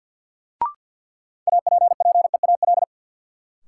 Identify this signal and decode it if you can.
Morse code, heard as a keyed tone, about 53 wpm, 700 Hz. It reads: APPEAL
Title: Sentences from Top 1300 Words - New Word Drill - 50wpm